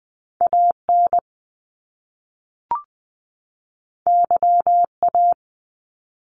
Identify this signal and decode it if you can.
Morse code, heard as a keyed tone, about 20 wpm, 700 Hz. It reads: AN YA
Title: Top Two Letter Combinations in Words 20wpm